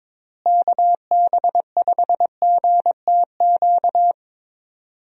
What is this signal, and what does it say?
Morse code, heard as a keyed tone, about 22 wpm, 700 Hz. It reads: KB5GTQ
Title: USA Call Signs - 6 Characters - Repeated 2x - 22wpm